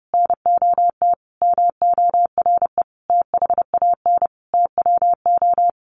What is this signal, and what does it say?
Morse code, heard as a keyed tone, about 30 wpm, 700 Hz. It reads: NOT MORE THAN TWO
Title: A Sound Of Thunder 30wpm